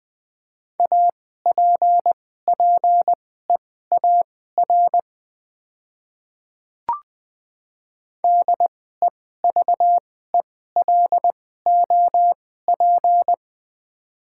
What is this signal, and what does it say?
Morse code, heard as a keyed tone, about 20 wpm, spaced as for 14 wpm, 700 Hz. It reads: APPEAR DEVELOP
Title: 2x Character Spacing - Top 500 Words - 20wpm